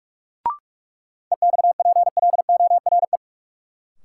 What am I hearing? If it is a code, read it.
Morse code, heard as a keyed tone, about 47 wpm, 700 Hz. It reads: EXPLORE
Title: Sentences from Top 1100 Words - New Word Drill - 45wpm